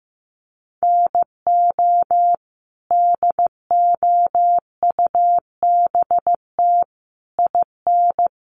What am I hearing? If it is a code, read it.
Morse code, heard as a keyed tone, about 15 wpm, 700 Hz. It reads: NO DOUBT IN
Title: Sets of 5 Words Encore2 15wpm